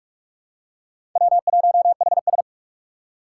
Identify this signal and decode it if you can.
Morse code, heard as a keyed tone, about 46 wpm, 700 Hz. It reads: W1HS
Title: USA Call Signs - 3x Character Spacing - 45wpm